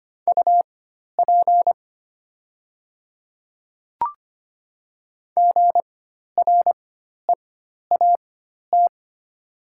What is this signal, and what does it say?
Morse code, heard as a keyed tone, about 25 wpm, spaced as for 12 wpm, 700 Hz. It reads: UP GREAT
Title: 4x Character Spacing - Top 200 Words - 25wpm